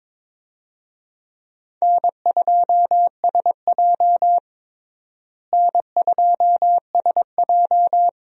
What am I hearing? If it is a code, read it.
Morse code, heard as a keyed tone, about 21 wpm, 700 Hz. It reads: N2SJ N2SJ
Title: USA Call Signs - 4 Characters - Repeated 2x - 22wpm